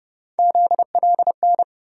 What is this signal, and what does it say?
Morse code, heard as a keyed tone, about 30 wpm, 700 Hz. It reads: ZLN
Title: USA Call Sign Postfixes - 3 Letters 30wpm